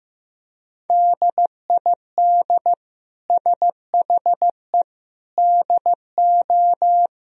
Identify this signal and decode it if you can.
Morse code, heard as a keyed tone, about 15 wpm, 700 Hz. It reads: DID SHE DO
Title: Sentences from Top 1000 Words - Review Part 3 - 15wpm